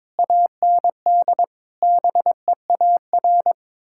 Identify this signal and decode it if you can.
Morse code, heard as a keyed tone, about 22 wpm, 700 Hz. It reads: AND BEAR
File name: practice / 22wpm / Sets of 8 Words - 22wpm